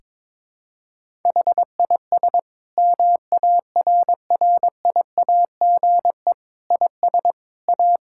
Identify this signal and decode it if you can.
Morse code, heard as a keyed tone, about 22 wpm, 700 Hz. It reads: HIS MARRIAGE IS A
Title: Sentences from Top 1000 Words - Review Part 6 - 22wpm